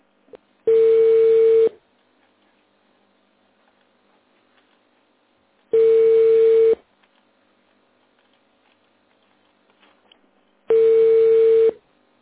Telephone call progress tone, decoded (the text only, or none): ringback tone